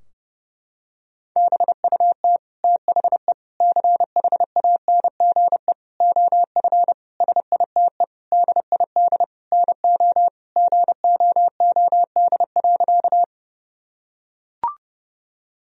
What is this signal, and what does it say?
Morse code, heard as a keyed tone, about 30 wpm, 700 Hz. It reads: BUT THE CHANGE OF SITE DID NO GOOD.
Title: Sentences from Top 1000 Words - Review Part 4 - 30wpm